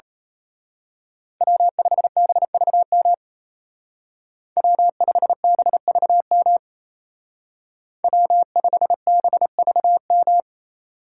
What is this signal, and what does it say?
Morse code, heard as a keyed tone, about 33 wpm, 700 Hz. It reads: W5BVM W5BVM W5BVM
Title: Speed Racing - USA 1x3 Call Signs 28wpm